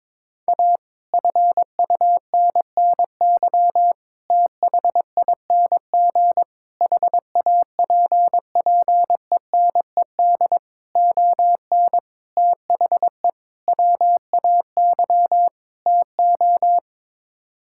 Morse code, heard as a keyed tone, about 22 wpm, 700 Hz. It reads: A FUNNY THING HAPPENED ON THE WAY TO
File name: practice / 22wpm / Sets of 8 Words - 22wpm